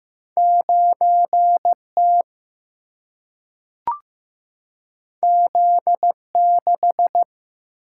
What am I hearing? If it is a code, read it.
Morse code, heard as a keyed tone, about 15 wpm, 700 Hz. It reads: 9T Z6
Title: Single-Letter Single-Number 15wpm